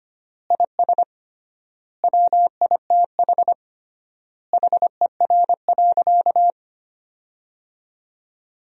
Morse code, heard as a keyed tone, about 25 wpm, 700 Hz. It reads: IS WITH HER.
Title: 3x Word-Spacing - Sentences from Top 100 Words - 25wpm